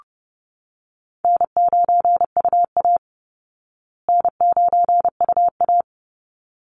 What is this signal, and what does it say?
Morse code, heard as a keyed tone, about 30 wpm, 700 Hz. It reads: N9UA N9UA
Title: Call Signs Repeated 3x - 30wpm